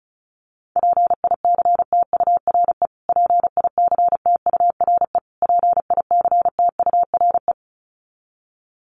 Morse code, heard as a keyed tone, about 34 wpm, 700 Hz. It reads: PICTURE PICTURE PICTURE
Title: Top 500 Words Repeated 35wpm